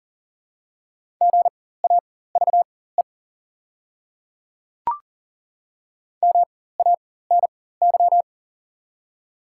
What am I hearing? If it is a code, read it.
Morse code, heard as a keyed tone, about 38 wpm, spaced as for 18 wpm, 690 Hz. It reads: GAVE MANY
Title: 4x Character Spacing - Top 500 Words - 40wpm